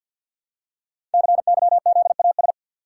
Morse code, heard as a keyed tone, about 50 wpm, 700 Hz. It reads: CYZAS
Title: Code Groups - 5 Letters - 50wpm